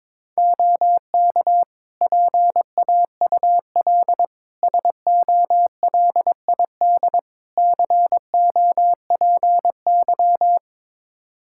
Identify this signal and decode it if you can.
Morse code, heard as a keyed tone, about 22 wpm, 700 Hz. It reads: OK PAUL SOLID COPY
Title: K1USN SST QSO Practice 22wpm